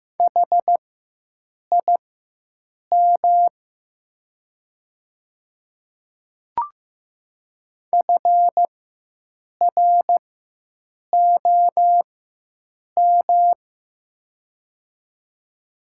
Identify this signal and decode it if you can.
Morse code, heard as a keyed tone, about 15 wpm, spaced as for 7 wpm, 700 Hz. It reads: HIM FROM
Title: 4x Character Spacing - Top 300 Words - 15wpm